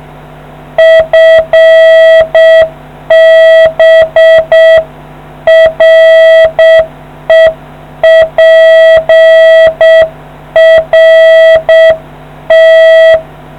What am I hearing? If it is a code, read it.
Morse code, heard as a keyed tone, about 6 wpm, 650 Hz. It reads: FB REPRT